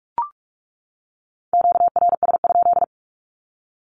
Morse code, heard as a keyed tone, about 44 wpm, 700 Hz. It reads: QRS?
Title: Q-Codes 45wpm